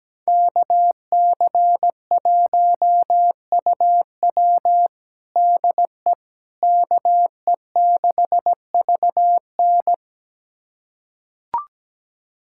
Morse code, heard as a keyed tone, about 17 wpm, 700 Hz. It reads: KC1UW DE KE6VN
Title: USA Callsign DE Callsign - 5 Characters - 17wpm